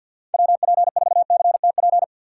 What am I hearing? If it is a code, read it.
Morse code, heard as a keyed tone, about 51 wpm, 700 Hz. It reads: WP4CTP
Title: USA Call Signs - 6 Characters - 4x Character Spacing - 50wpm